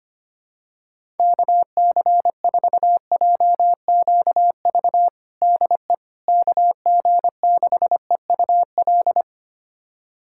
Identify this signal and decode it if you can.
Morse code, heard as a keyed tone, about 25 wpm, 700 Hz. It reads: KC4JQV DE KG6EUL
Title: USA Callsign DE Callsign - 6 Characters - 25wpm